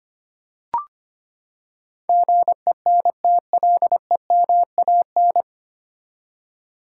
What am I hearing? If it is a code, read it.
Morse code, heard as a keyed tone, about 25 wpm, 700 Hz. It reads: GENTLEMAN
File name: Nine Letter Words - 25wpm